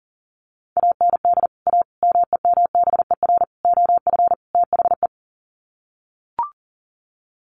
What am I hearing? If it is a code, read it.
Morse code, heard as a keyed tone, about 40 wpm, 700 Hz. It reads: AND A MEMBER OF THE
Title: Sets of 5 Words 40wpm